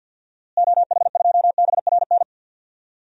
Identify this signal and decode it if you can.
Morse code, heard as a keyed tone, about 52 wpm, 700 Hz. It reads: OH2BRN
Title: DXCC-100 Call Signs - 6 Characters - Repeated 3x - 50wpm